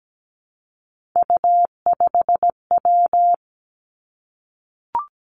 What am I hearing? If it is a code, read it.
Morse code, heard as a keyed tone, about 17 wpm, 700 Hz. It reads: U5W
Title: Three Challenging Letter-Number Combinations 17wpm